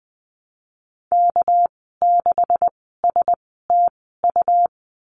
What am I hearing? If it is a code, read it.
Morse code, heard as a keyed tone, about 20 wpm, spaced as for 15 wpm, 700 Hz. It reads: K6STU
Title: 207 Lesson 33g - Letter j - Call Signs 20wpm